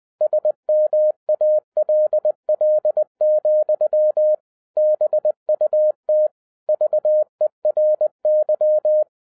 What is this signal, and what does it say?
Morse code, heard as a keyed tone, about 20 wpm, 610 Hz. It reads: SMALL, BUT VERY
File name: 2011-07-mp3 / n08 A LOT HAS CHANGED 20wpm